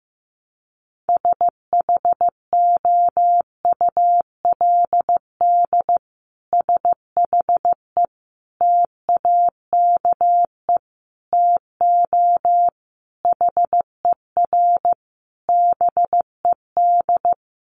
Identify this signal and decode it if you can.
Morse code, heard as a keyed tone, about 15 wpm, 700 Hz. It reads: SHOULD SHE TAKE TO HER BED
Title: Sentences from Top 1200 Words 15wpm